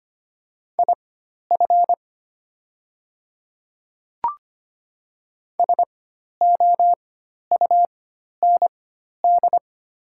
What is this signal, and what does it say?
Morse code, heard as a keyed tone, about 25 wpm, spaced as for 12 wpm, 700 Hz. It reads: IF SOUND
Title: 4x Character Spacing - Top 100 Words - 25wpm